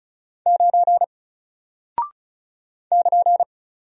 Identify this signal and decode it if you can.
Morse code, heard as a keyed tone, about 36 wpm, 700 Hz. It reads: 9 <KN>
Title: Single Character (Letter:Number:Punctuation:Prosign) Alt 35wpm